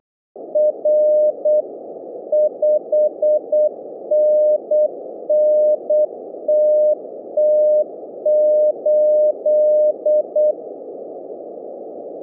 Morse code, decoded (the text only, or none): R 5NN TT8